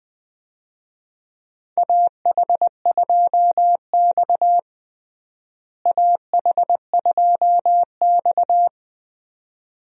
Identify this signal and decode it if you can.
Morse code, heard as a keyed tone, about 20 wpm, 700 Hz. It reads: AH2X AH2X
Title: USA Call Signs - 4 Characters - Repeated 2x - 20wpm